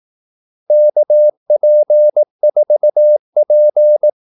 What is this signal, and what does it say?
Morse code, heard as a keyed tone, about 18 wpm, 600 Hz. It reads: KP4P
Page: Kaldesignaler 18 wpm